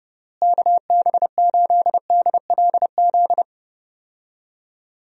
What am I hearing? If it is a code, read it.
Morse code, heard as a keyed tone, about 32 wpm, 700 Hz. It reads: KB8DLZ
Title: USA Call Signs - 6 Characters - Repeated 3x - 30wpm